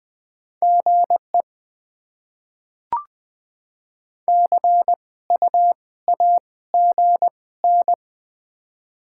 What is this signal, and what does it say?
Morse code, heard as a keyed tone, about 21 wpm, spaced as for 16 wpm, 690 Hz. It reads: GE CU AGN
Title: Top 20 QSO Elements - 2x Character Spacing - 20wpm